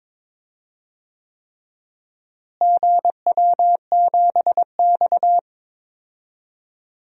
Morse code, heard as a keyed tone, about 22 wpm, 700 Hz. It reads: GW7X
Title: DXCC-100 Call Signs - 4 Characters - Repeated 3x - 22wpm